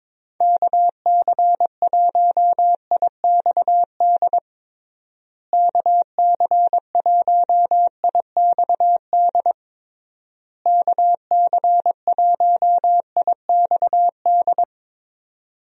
Morse code, heard as a keyed tone, about 22 wpm, 700 Hz. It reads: KC1IXD KC1IXD KC1IXD
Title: USA Call Signs - 6 Characters - Repeated 3x - 22wpm